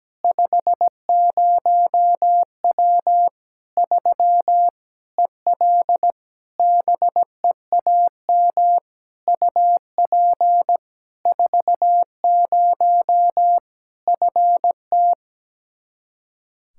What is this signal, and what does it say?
Morse code, heard as a keyed tone, about 17 wpm, 700 Hz. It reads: 50W 3 EL BEAM UP 40 FT
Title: Practice QSO - Exchange 2 - Info - 17wpm